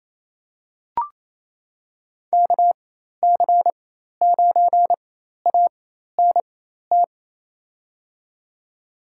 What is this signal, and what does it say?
Morse code, heard as a keyed tone, about 28 wpm, spaced as for 13 wpm, 700 Hz. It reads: KC9ANT
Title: USA Call Signs - 4x Character Spacing - 28wpm